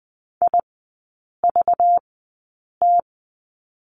Morse code, heard as a keyed tone, about 20 wpm, spaced as for 8 wpm, 690 Hz. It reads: IVT
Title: 191 Lesson 31e - Letter v - Three Characters 20wpm